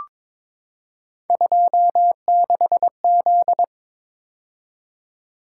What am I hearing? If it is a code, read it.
Morse code, heard as a keyed tone, about 22 wpm, 700 Hz. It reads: 26Z
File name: practice / 22wpm / Three Letter-Number Combinations - 22wpm